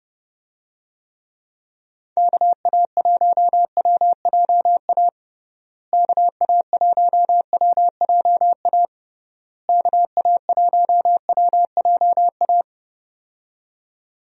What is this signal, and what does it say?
Morse code, heard as a keyed tone, about 30 wpm, 700 Hz. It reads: KA1WJA KA1WJA KA1WJA
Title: USA Call Signs - 6 Characters - Repeated 3x - 30wpm